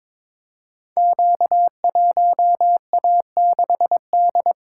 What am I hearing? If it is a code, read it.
Morse code, heard as a keyed tone, about 22 wpm, 700 Hz. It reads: Q1A6D
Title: Code Groups - 5 Letter:Numbers - 22wpm